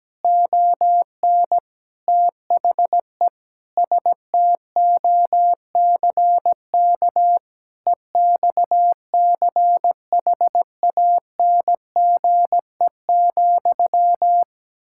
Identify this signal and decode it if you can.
Morse code, heard as a keyed tone, about 17 wpm, 700 Hz. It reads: ON THE STOCK EXCHANGE,
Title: Sentences from Top 1000 Words - Review Part 6 - 17wpm